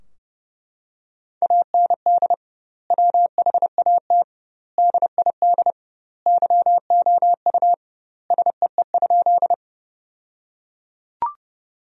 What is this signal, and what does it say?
Morse code, heard as a keyed tone, about 29 wpm, 700 Hz. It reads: AND WHAT DID YOU SEE?
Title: 2x Word-Spacing - Sentences from Top 100 Words - 30wpm